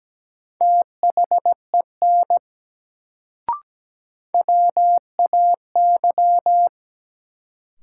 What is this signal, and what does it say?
Morse code, heard as a keyed tone, about 17 wpm, 700 Hz. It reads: THEN WAY
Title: Sets of 2 Words - New Word Drill - 17wpm